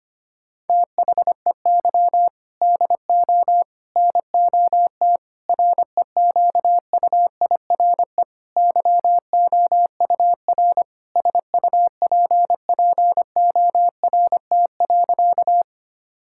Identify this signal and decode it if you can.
Morse code, heard as a keyed tone, about 25 wpm, 700 Hz. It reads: THEY DO NOT REQUIRE YOUR SUPPORT.
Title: Sentences from Top 1000 Words - Review Part 6 - 25wpm